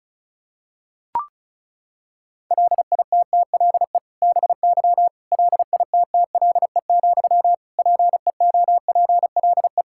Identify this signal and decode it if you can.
Morse code, heard as a keyed tone, about 36 wpm, 700 Hz. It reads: LITTLE BY LITTLE, PEOPLE
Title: Sentences from Top 1600 Words 35wpm